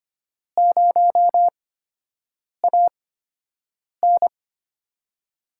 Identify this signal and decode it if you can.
Morse code, heard as a keyed tone, about 25 wpm, spaced as for 7 wpm, 700 Hz. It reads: 0AN
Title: USA Call Signs - 8x Character Spacing - 25wpm